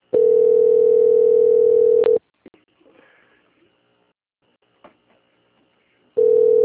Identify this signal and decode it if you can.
Ringback tone (North America), 440 + 480 Hz.
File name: out- - - - - .850